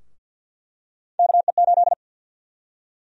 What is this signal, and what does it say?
Morse code, heard as a keyed tone, about 49 wpm, 700 Hz. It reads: KE8
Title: USA Call Sign Prefixes - Repeated 2x - 50wpm